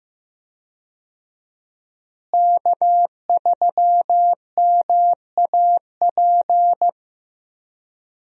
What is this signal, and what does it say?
Morse code, heard as a keyed tone, about 15 wpm, 700 Hz. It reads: K3MAP
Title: USA Call Signs - 5 Characters - Repeated 2x - 15wpm